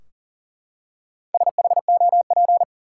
Morse code, heard as a keyed tone, about 40 wpm, 700 Hz. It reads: SHOP
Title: Four Letter Words - 40wpm